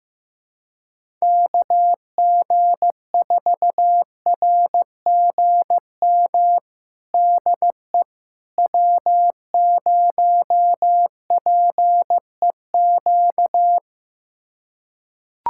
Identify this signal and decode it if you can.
Morse code, heard as a keyed tone, about 15 wpm, 700 Hz. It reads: KG4RGM DE W0PEQ
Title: USA Callsign DE Callsign 15wpm